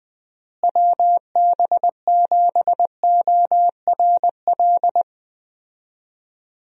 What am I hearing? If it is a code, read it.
Morse code, heard as a keyed tone, about 21 wpm, 700 Hz. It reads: WB7ORL
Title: USA Call Signs - 6 Characters - Repeated 3x - 20wpm